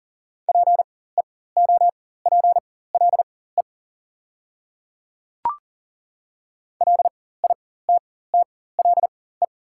Morse code, heard as a keyed tone, about 40 wpm, spaced as for 19 wpm, 700 Hz. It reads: PEOPLE LITTLE
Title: 4x Character Spacing - Top 200 Words - 40wpm